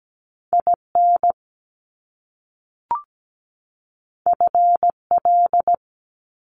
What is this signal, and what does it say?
Morse code, heard as a keyed tone, about 17 wpm, 700 Hz. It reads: IN FL
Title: US States 17wpm